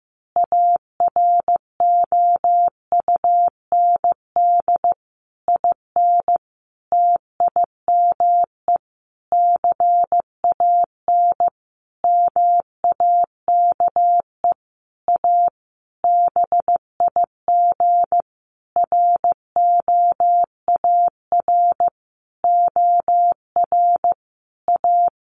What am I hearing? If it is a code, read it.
Morse code, heard as a keyed tone, about 15 wpm, 700 Hz. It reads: AROUND IN TIME CAN MAKE A BIG ROAR OR A
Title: A Sound Of Thunder 15wpm